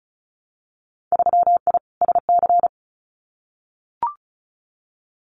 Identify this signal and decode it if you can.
Morse code, heard as a keyed tone, about 34 wpm, 700 Hz. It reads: 3I SC
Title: Winter Field Day Exchanges 35wpm